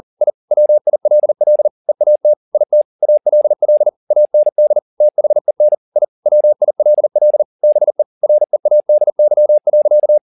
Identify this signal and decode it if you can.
Morse code, heard as a keyed tone, about 40 wpm, 600 Hz. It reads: I WILL EAT IT ALL AND THEN I WILL BE READY.